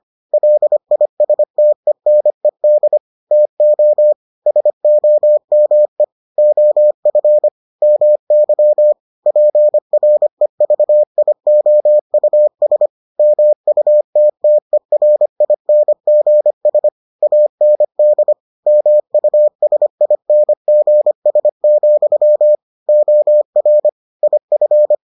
Morse code, heard as a keyed tone, about 25 wpm, 600 Hz. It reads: LISTENED TO SOME OF MY PREVIOUS MUTTERINGS AND MUSINGS, OR IF